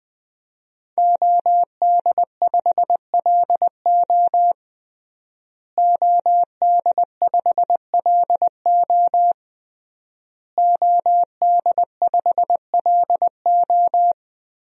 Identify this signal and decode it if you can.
Morse code, heard as a keyed tone, about 20 wpm, 700 Hz. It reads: OD5LO OD5LO OD5LO
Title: DXCC-100 Call Signs - 5 Characters - Repeated 3x - 20wpm